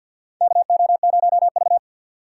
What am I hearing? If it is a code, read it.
Morse code, heard as a keyed tone, about 50 wpm, 690 Hz. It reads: KO0V
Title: USA Call Signs - 3x Character Spacing - 50wpm